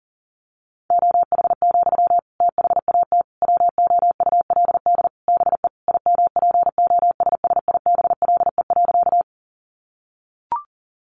Morse code, heard as a keyed tone, about 40 wpm, 700 Hz. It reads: OH, THAT WOULD BE IMPOSSIBLE.
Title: Sentences from Top 1100 Words 40wpm